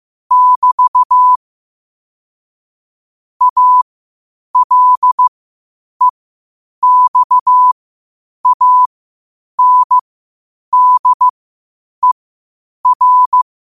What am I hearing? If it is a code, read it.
Morse code, heard as a keyed tone, about 15 wpm, spaced as for 8 wpm, 1000 Hz. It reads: <BT> ALEXANDER
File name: 8wpm